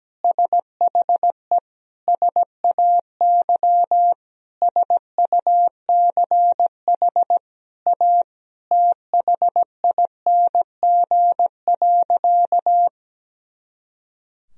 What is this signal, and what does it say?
Morse code, heard as a keyed tone, about 17 wpm, 700 Hz. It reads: SHE SAY SUCH A THING.
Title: Sentences from Top 200 Words and Abbreviations 17wpm